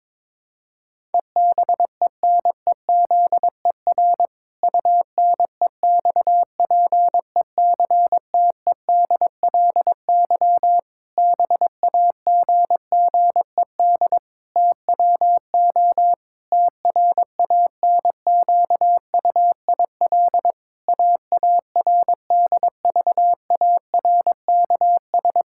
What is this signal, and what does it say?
Morse code, heard as a keyed tone, about 21 wpm, 700 Hz. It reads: EBENEZER UNEXPECTEDLY BAGGED TWO TRANQUIL AARDVARKS
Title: Pangram Challenge v2 - 22wpm